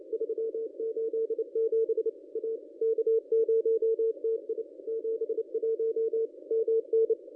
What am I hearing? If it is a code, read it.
Morse code, heard as a keyed tone, about 28 wpm, 450 Hz. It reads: 387 A K0TI 71 MN
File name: contests / ss-cw-k5tr-op-n5rz -2200